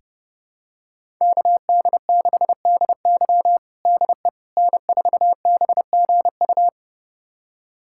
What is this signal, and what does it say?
Morse code, heard as a keyed tone, about 32 wpm, 700 Hz. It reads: KD6DY DE N4BGU
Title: USA Callsign DE Callsign - 5 Characters - 30wpm